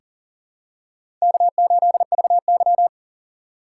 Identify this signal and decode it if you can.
Morse code, heard as a keyed tone, about 40 wpm, 700 Hz. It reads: K8VY
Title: USA Call Signs - 6x Character Spacing - 40wpm